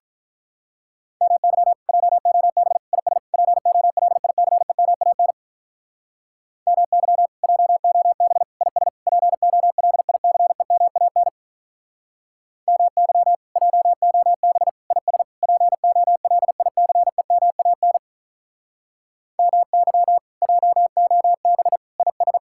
Morse code, heard as a keyed tone, about 47 wpm, 700 Hz. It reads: MY JOB IS POLICEMAN MY JOB IS POLICEMAN MY JOB IS POLICEMAN MY JOB IS